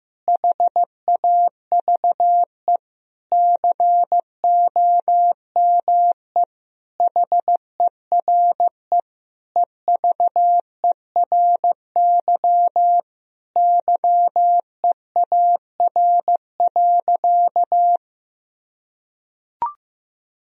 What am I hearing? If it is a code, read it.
Morse code, heard as a keyed tone, about 16 wpm, 700 Hz. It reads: HAVE COME HERE EVERY YEAR.
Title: Sentences from Top 1000 Words - Review Part 1 - 15wpm